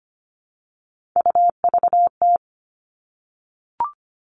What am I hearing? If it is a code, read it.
Morse code, heard as a keyed tone, about 25 wpm, 700 Hz. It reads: UVT
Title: All Cut Numbers - 3-Digit 25wpm